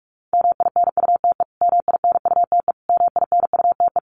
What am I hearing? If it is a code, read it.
Morse code, heard as a keyed tone, about 44 wpm, 700 Hz. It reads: MINUTE MINUTE MINUTE
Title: Top 500 Words Repeated 45wpm